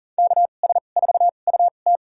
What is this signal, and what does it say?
Morse code, heard as a keyed tone, about 40 wpm, spaced as for 29 wpm, 700 Hz. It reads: KS4UT
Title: USA Call Signs - 2x Character Spacing - 40wpm